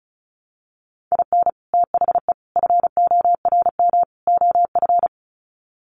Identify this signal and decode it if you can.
Morse code, heard as a keyed tone, about 34 wpm, 700 Hz. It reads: IN THE FORM OF
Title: Sets of 4 Words 35wpm